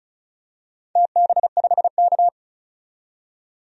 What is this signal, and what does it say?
Morse code, heard as a keyed tone, about 34 wpm, 700 Hz. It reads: TB5K
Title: DXCC-100 Call Signs - 4 Characters - 35wpm